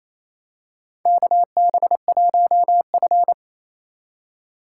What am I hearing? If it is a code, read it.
Morse code, heard as a keyed tone, about 28 wpm, 700 Hz. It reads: KB1F
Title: USA Call Signs - 4 Characters - 4x Character Spacing - 28wpm